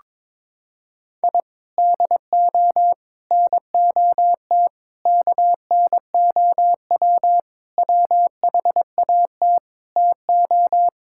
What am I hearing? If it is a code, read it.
Morse code, heard as a keyed tone, about 22 wpm, 700 Hz. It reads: I DO NOT KNOW WHAT TO
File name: practice / 22wpm / Sentences with 4-Word Phrases - 22wpm